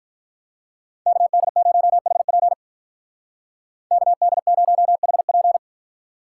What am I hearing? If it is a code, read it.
Morse code, heard as a keyed tone, about 44 wpm, 700 Hz. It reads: KD0HP KD0HP
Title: Speed Racing - USA Call Signs 35wpm